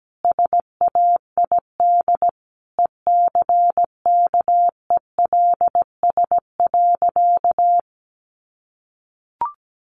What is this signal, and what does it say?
Morse code, heard as a keyed tone, about 17 wpm, 700 Hz. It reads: SAID ECKELS.
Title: A Sound Of Thunder 17wpm